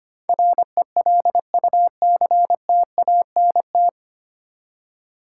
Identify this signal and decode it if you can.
Morse code, heard as a keyed tone, about 25 wpm, 700 Hz. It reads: RELUCTANT
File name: Nine Letter Words - 25wpm